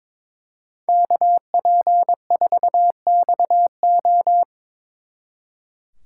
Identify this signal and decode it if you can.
Morse code, heard as a keyed tone, about 26 wpm, 700 Hz. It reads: KP4XO
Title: Speed Racing - USA 2x2 Call Signs 22wpm